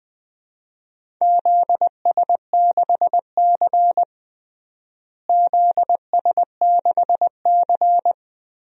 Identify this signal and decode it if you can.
Morse code, heard as a keyed tone, about 20 wpm, 700 Hz. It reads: ZS6C ZS6C
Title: DXCC-100 Call Signs - 4 Characters - Repeated 3x - 20wpm